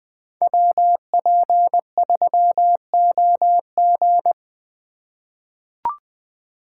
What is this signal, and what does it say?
Morse code, heard as a keyed tone, about 20 wpm, 700 Hz. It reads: WP3OG
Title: USA Call Signs - 5 Characters - 2x Character Spacing - 20wpm